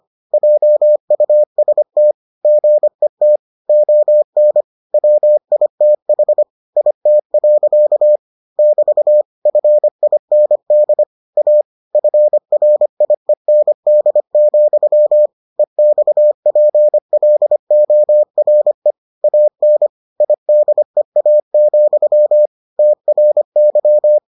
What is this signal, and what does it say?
Morse code, heard as a keyed tone, about 25 wpm, 600 Hz. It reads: JUST GET ON WITH IT. <BT> FIND A FRIEND, EXPLORE AN IDEA, TRY